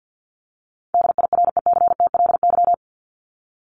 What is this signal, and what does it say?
Morse code, heard as a keyed tone, about 50 wpm, 700 Hz. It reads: DIRECTLY
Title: LY Suffix - Top 100 Words with Base Words 50wpm